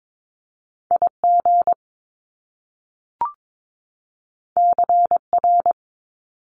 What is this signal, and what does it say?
Morse code, heard as a keyed tone, about 22 wpm, 700 Hz. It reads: IG CR
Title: Top Two Letter Combinations in Words 22wpm